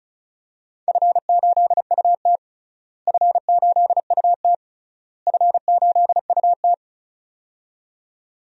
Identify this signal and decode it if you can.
Morse code, heard as a keyed tone, about 36 wpm, 700 Hz. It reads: F8UT F8UT F8UT
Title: DXCC-100 Call Signs - 4 Characters - Repeated 3x - 35wpm